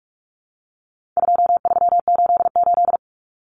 Morse code, heard as a keyed tone, about 44 wpm, 700 Hz. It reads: 2388
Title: Numbers - 4-Digit 45wpm